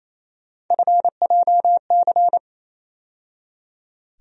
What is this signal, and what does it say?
Morse code, heard as a keyed tone, about 27 wpm, 700 Hz. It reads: FJC
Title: USA Call Sign Postfixes - 3 Letters 28wpm